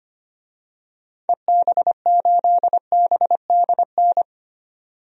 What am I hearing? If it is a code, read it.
Morse code, heard as a keyed tone, about 25 wpm, 700 Hz. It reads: EB8BDN
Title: DXCC-100 Call Signs - 6 Characters - 25wpm